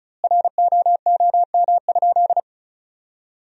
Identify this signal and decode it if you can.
Morse code, heard as a keyed tone, about 36 wpm, 700 Hz. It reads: ROOM?
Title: 2x Word-Spacing - Sentences from Top 300 Words - 35wpm